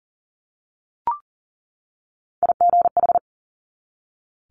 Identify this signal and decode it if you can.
Morse code, heard as a keyed tone, about 38 wpm, 700 Hz. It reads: IGH
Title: Multi-letter Phonemes 40wpm